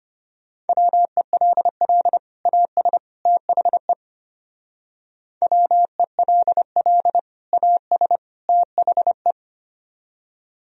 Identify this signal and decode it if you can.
Morse code, heard as a keyed tone, about 33 wpm, 700 Hz. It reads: WELL AS THE WELL AS THE